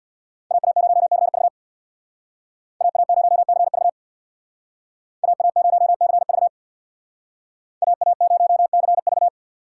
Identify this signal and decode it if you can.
Morse code, heard as a keyed tone, about 63 wpm, 700 Hz. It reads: AA0CV AA0CV AA0CV AA0CV